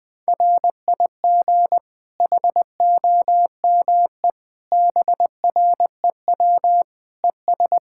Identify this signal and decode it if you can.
Morse code, heard as a keyed tone, about 20 wpm, 700 Hz. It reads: RIG HOME BREW ES
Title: K1USN SST QSO Practice 20wpm